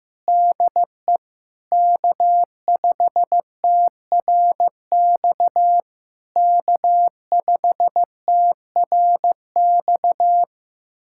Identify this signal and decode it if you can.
Morse code, heard as a keyed tone, about 16 wpm, 700 Hz. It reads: DE K5TRX K5TRX
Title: SOTA QSO Practice 15wpm